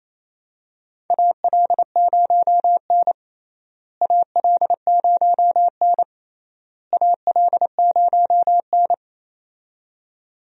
Morse code, heard as a keyed tone, about 28 wpm, 700 Hz. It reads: AL0N AL0N AL0N
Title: USA Call Signs - 4 Characters - Repeated 3x - 28wpm